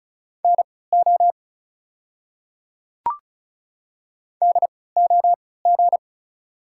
Morse code, heard as a keyed tone, about 35 wpm, spaced as for 20 wpm, 690 Hz. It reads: NO DOG
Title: 3x Character Spacing - Top 500 Words - 35wpm